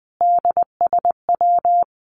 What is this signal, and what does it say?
Morse code, heard as a keyed tone, about 20 wpm, 700 Hz. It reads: DSW
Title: Abbreviations Repeated 3x - 20wpm